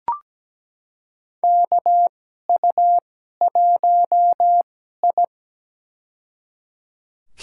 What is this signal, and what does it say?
Morse code, heard as a keyed tone, about 17 wpm, spaced as for 12 wpm, 700 Hz. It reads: KU1I
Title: USA Call Signs - 4 Characters - 2x Character Spacing - 17wpm